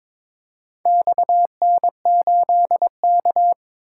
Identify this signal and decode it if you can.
Morse code, heard as a keyed tone, about 22 wpm, 700 Hz. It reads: XN8K
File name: practice / 22wpm / Four Letter-Number Combinations - 22wpm